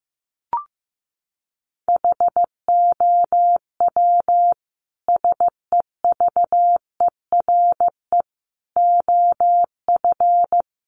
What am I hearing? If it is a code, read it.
Morse code, heard as a keyed tone, about 14 wpm, 700 Hz. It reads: HOW SEVERE OF
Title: Sentences from Top 1400 Words 15wpm